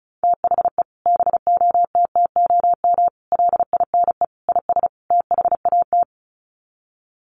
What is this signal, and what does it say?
Morse code, heard as a keyed tone, about 34 wpm, 700 Hz. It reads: THE BOTTOM LINE IS THAT
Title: Sets of 5 Words Encore 35wpm